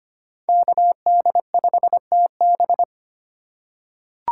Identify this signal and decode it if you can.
Morse code, heard as a keyed tone, about 25 wpm, 700 Hz. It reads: KD5TB
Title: USA Call Signs - 5 Characters - 2x Character Spacing - 25wpm